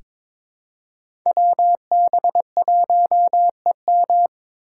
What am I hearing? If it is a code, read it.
Morse code, heard as a keyed tone, about 22 wpm, 700 Hz. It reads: WB1EM
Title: Code Groups - 5 Letter:Numbers - 22wpm